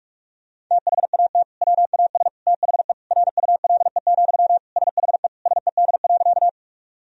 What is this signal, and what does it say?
Morse code, heard as a keyed tone, about 46 wpm, 700 Hz. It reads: THAT WAS THE RULE, SHE SED.
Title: Sentences from Top 400 Words and Abbreviations 45wpm